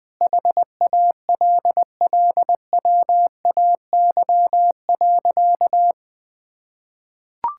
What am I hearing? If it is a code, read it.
Morse code, heard as a keyed tone, about 20 wpm, 700 Hz. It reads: HALLWAY.
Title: Sentences with 4-Word Phrases - Spoken Beginning - 20wpm